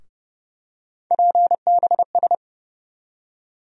Morse code, heard as a keyed tone, about 31 wpm, 700 Hz. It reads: PBS
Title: USA Call Sign Postfixes - 3 Letters 30wpm